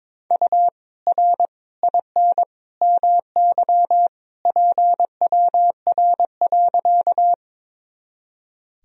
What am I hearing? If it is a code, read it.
Morse code, heard as a keyed tone, about 22 wpm, 700 Hz. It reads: U R IN MY PWR.
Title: Sentences from Top 500 Words and Abbreviations 22wpm